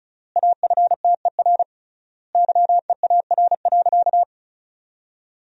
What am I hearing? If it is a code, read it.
Morse code, heard as a keyed tone, about 36 wpm, 700 Hz. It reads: AFTER YEAR.
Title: 3x Word-Spacing - Sentences from Top 300 Words - 35wpm